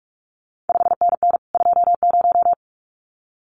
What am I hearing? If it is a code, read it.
Morse code, heard as a keyed tone, about 46 wpm, 700 Hz. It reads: 5NN 20
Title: CQ WW DX Contest Exchange - 45wpm